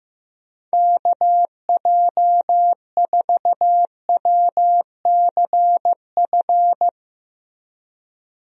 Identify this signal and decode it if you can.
Morse code, heard as a keyed tone, about 14 wpm, 700 Hz. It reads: KJ4WCF
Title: USA Call Signs - 6 Characters - Repeated 3x - 15wpm